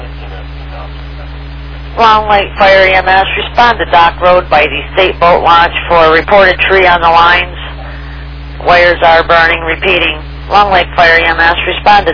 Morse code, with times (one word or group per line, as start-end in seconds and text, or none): none